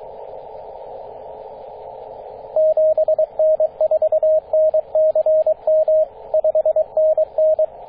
Morse code, heard as a keyed tone, about 22 wpm, 630 Hz, with radio noise. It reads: 7N4NCM 5NN
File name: 9m0o-24-891mhz-cw